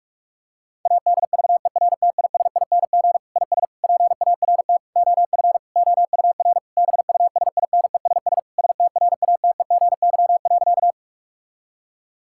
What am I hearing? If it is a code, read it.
Morse code, heard as a keyed tone, about 41 wpm, 700 Hz. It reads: ADVERTISING IS PART OF OUR BUSINESS STRATEGY.
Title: Sentences from Top 1000 Words - Review Part 6 - 45wpm